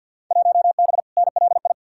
Morse code, heard as a keyed tone, about 53 wpm, 700 Hz. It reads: 1B NLI
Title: Field Day Exchanges Encore - 50wpm